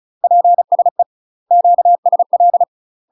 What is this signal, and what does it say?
Morse code, heard as a keyed tone, about 34 wpm, 700 Hz. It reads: PSE QSL
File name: UK MASTER QSOs TRAINING / 35WPM / UK MASTER QSOs 0005